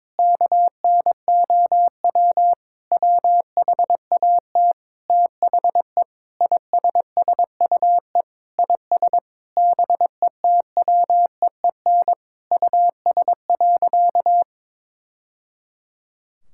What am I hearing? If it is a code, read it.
Morse code, heard as a keyed tone, about 22 wpm, 700 Hz. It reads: KNOW WHAT THE ISSUE IS BETWEEN US.
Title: Sentences from Top 1000 Words - Review Part 6 - 22wpm